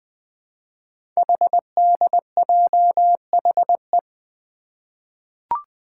Morse code, heard as a keyed tone, about 20 wpm, 700 Hz. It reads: HDJHE
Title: Code Groups - 5 Letters - 20wpm